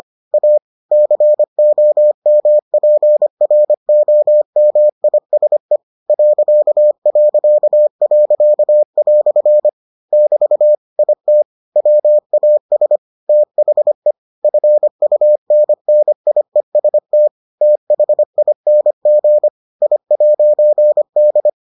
Morse code, heard as a keyed tone, about 25 wpm, 600 Hz. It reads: A COMPROMISE ..." <BT> IT WAS THE FUNNIEST THING I'D